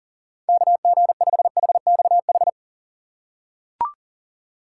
Morse code, heard as a keyed tone, about 40 wpm, 700 Hz. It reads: KG5HXH
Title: USA Call Signs - 6 Characters - 2x Character Spacing - 40wpm